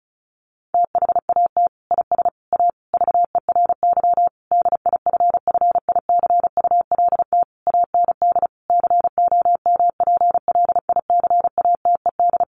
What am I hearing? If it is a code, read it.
Morse code, heard as a keyed tone, about 34 wpm, 700 Hz. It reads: THAT IS A VERY DIFFICULT AND COMPLICATED
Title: Sentences from Top 1200 Words 35wpm